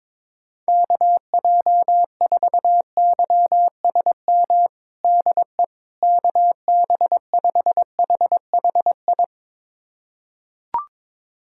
Morse code, heard as a keyed tone, about 21 wpm, 700 Hz. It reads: KJ4YSM DE KB5HHI
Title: USA Callsign DE Callsign - 6 Characters - 22wpm